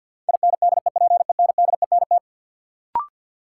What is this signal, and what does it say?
Morse code, heard as a keyed tone, about 50 wpm, 700 Hz. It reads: INDEPENDENT
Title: Sentences from Top 900 Words - New Word Drill - 50wpm